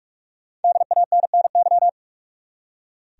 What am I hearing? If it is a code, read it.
Morse code, heard as a keyed tone, about 45 wpm, 700 Hz. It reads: DANNY
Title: USA Common Names - 45wpm